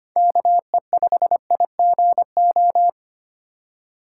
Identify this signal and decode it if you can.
Morse code, heard as a keyed tone, about 25 wpm, 700 Hz. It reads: KE5IGO
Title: USA Call Signs - 6 Characters - 3x Character Spacing - 25wpm